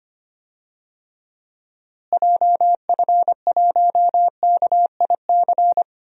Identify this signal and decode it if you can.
Morse code, heard as a keyed tone, about 25 wpm, 700 Hz. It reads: JF1KIC
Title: DXCC-100 Call Signs - 6 Characters - Repeated 3x - 25wpm